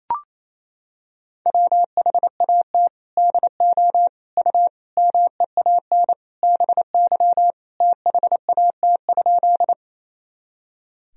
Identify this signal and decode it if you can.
Morse code, heard as a keyed tone, about 27 wpm, 700 Hz. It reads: WHAT DO U MEAN BY THAT?
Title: Sentences from Top 200 Words and Abbreviations 28wpm